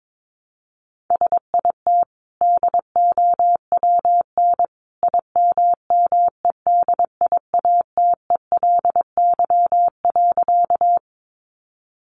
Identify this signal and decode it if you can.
Morse code, heard as a keyed tone, about 22 wpm, 700 Hz. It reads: SIT DOWN IMMEDIATELY.
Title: Sentences from Top 1100 Words 22wpm